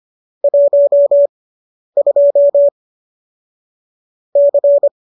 Morse code, heard as a keyed tone, about 25 wpm, spaced as for 10 wpm, 570 Hz. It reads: 12 C